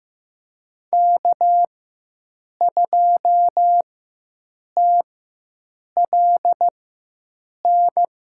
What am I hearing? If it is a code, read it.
Morse code, heard as a keyed tone, about 15 wpm, spaced as for 7 wpm, 700 Hz. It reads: K2TLN
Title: USA Call Signs - 4x Character Spacing - 15wpm